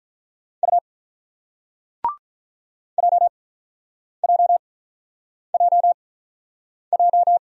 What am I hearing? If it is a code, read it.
Morse code, heard as a keyed tone, about 47 wpm, spaced as for 9 wpm, 700 Hz. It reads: U JJJJ